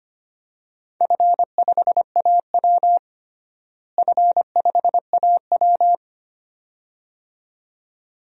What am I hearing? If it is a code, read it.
Morse code, heard as a keyed tone, about 25 wpm, 700 Hz. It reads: F5AW F5AW
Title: DXCC-100 Call Signs - 4 Characters - Repeated 2x - 25wpm